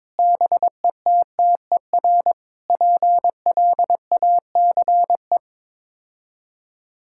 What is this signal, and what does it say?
Morse code, heard as a keyed tone, about 21 wpm, 700 Hz. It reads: BETTER PLACE
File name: practice / 22wpm / Sets of 7 Words - 22wpm